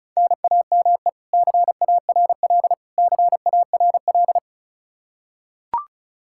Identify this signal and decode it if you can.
Morse code, heard as a keyed tone, about 34 wpm, 700 Hz. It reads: NAME CARL CARL
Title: K1USN SST QSO Practice 35wpm